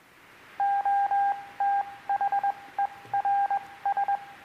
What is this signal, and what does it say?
Morse code, heard as a keyed tone, about 21 wpm, 800 Hz, with radio noise. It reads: O T H E R S